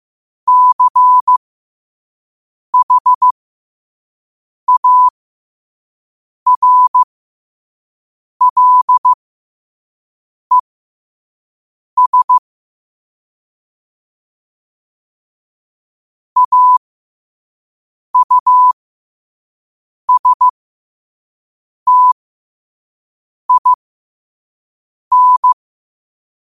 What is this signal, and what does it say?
Morse code, heard as a keyed tone, about 15 wpm, spaced as for 5 wpm, 1000 Hz. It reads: CHARLES AUSTIN
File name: qotdfiles / 5wpm